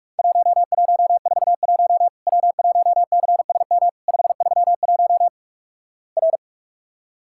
Morse code, heard as a keyed tone, about 46 wpm, 700 Hz. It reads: 1141 W1CSM 531 R
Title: Worked All Europe DX Contest - QTC Exchange - 45wpm